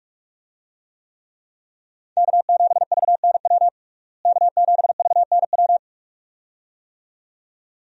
Morse code, heard as a keyed tone, about 42 wpm, 700 Hz. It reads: K7VNW K7VNW
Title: USA Call Signs - 5 Characters - Repeated 2x - 45wpm